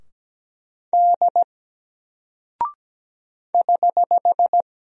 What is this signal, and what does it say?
Morse code, heard as a keyed tone, about 17 wpm, 700 Hz. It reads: D <HH>
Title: Single Character (Letter:Number:Punctuation:Prosign) Alt 17wpm